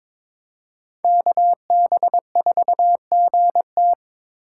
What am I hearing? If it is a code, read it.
Morse code, heard as a keyed tone, about 26 wpm, 700 Hz. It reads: KB4GT
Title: Speed Racing - USA 2x2 Call Signs 22wpm